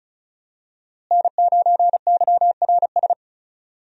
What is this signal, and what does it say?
Morse code, heard as a keyed tone, about 36 wpm, 700 Hz. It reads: N9YRS
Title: USA Call Signs - 5 Characters - Repeated 3x - 35wpm